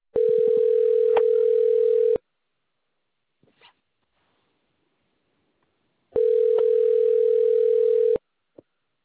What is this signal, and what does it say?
Ringback tone (North America), 440 + 480 Hz.